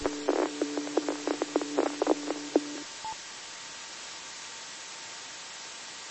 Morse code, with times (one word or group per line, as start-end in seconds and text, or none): none